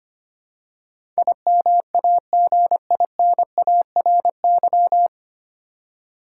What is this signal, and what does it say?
Morse code, heard as a keyed tone, about 25 wpm, 700 Hz. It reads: IMAGINARY
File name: practice / 25wpm / 9 Letter Words - First 3 Letters Given - 25wpm